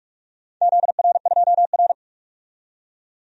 Morse code, heard as a keyed tone, about 47 wpm, 700 Hz. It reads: ZR2R
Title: DXCC-100 Call Signs - 4 Characters - 45wpm